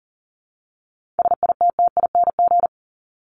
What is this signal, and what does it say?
Morse code, heard as a keyed tone, about 40 wpm, 700 Hz. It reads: SITTING
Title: ING Suffix - Top 100 Words with Base Words 40wpm